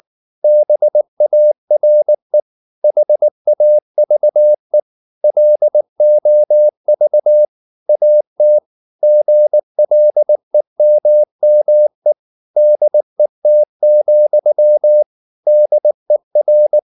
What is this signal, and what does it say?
Morse code, heard as a keyed tone, about 19 wpm, 600 Hz. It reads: BARE HAVE LOV AT GLEMME DET, DER
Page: Ferie 19 wpm